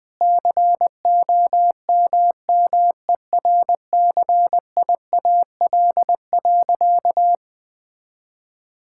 Morse code, heard as a keyed tone, about 20 wpm, 700 Hz. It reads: COMMERCIAL.
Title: Sentences from Top 1000 Words - Review Part 5 - 20wpm